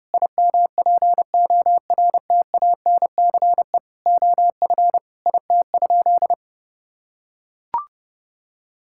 Morse code, heard as a keyed tone, about 30 wpm, 700 Hz. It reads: IMPORTANCE OF IT?
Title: Sentences from Top 1600 Words 30wpm